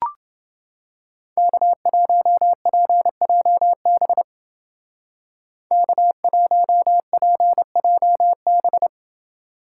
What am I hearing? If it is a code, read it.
Morse code, heard as a keyed tone, about 28 wpm, 700 Hz. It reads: K1PJB K1PJB